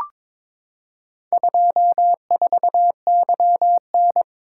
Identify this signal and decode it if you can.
Morse code, heard as a keyed tone, about 22 wpm, 700 Hz. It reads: 24YN
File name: practice / 22wpm / Four Letter-Number Combinations - 22wpm